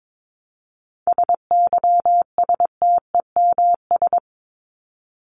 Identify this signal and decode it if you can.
Morse code, heard as a keyed tone, about 22 wpm, 700 Hz. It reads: SYSTEMS
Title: ES:S Suffix - Top 100 Words with Base Words 22wpm